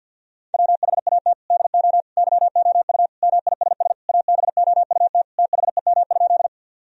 Morse code, heard as a keyed tone, about 50 wpm, 700 Hz. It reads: WHAT DO YOU MISS ABOUT THEM?
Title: Sentences from Top 1000 Words - Review Part 3 - 50wpm